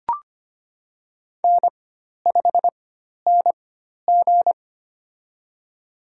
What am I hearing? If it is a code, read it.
Morse code, heard as a keyed tone, about 25 wpm, spaced as for 12 wpm, 700 Hz. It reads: N5NG
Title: USA Call Signs - 4 Characters - 4x Character Spacing - 25wpm